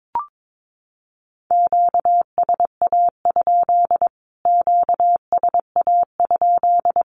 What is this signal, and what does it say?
Morse code, heard as a keyed tone, about 22 wpm, 700 Hz. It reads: QSA? QSA?
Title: Q-Codes Repeated 3x 22wpm